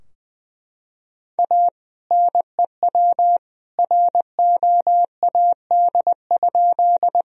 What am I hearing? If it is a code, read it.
Morse code, heard as a keyed tone, about 21 wpm, 700 Hz. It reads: A NEW ROAD?
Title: Sentences from Top 1000 Words - Review Part 2 - 20wpm